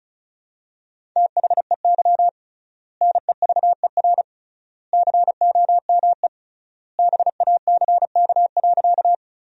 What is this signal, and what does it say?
Morse code, heard as a keyed tone, about 34 wpm, 700 Hz. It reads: THEY NEVER COME BACK.
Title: 3x Word-Spacing - Sentences from Top 300 Words - 35wpm